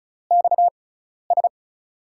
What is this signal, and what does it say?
Morse code, heard as a keyed tone, about 35 wpm, spaced as for 12 wpm, 700 Hz. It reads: XS
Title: USA Call Signs - 6 Characters - 6x Character Spacing - 35wpm